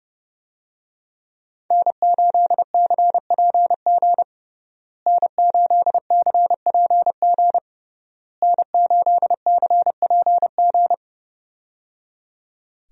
Morse code, heard as a keyed tone, about 30 wpm, 700 Hz. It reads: N8CPG N8CPG N8CPG
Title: USA Call Signs - 5 Characters - Repeated 3x - 30wpm